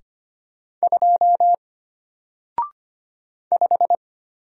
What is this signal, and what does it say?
Morse code, heard as a keyed tone, about 25 wpm, 700 Hz. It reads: 2 5
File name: Single-Letter-Number-25wpm